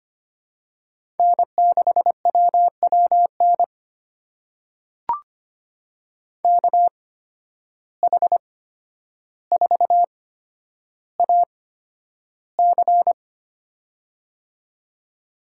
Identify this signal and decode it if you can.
Morse code, heard as a keyed tone, about 25 wpm, spaced as for 11 wpm, 700 Hz. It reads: N6WWN KH4AC
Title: USA Call Signs - 5 Characters - 8x Character Spacing - 25wpm